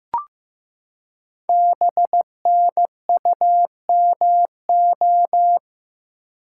Morse code, heard as a keyed tone, about 15 wpm, 700 Hz. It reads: BNUMO
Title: Code Groups - 5 Letters - 15wpm